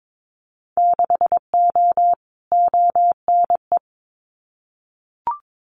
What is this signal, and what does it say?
Morse code, heard as a keyed tone, about 22 wpm, 700 Hz. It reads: 6O ONE
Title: Winter Field Day Exchanges Encore 22wpm